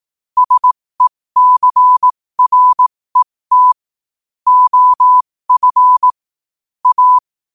Morse code, heard as a keyed tone, about 18 wpm, spaced as for 15 wpm, 1000 Hz. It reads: SECRET OF A
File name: qotdfiles / 15wpm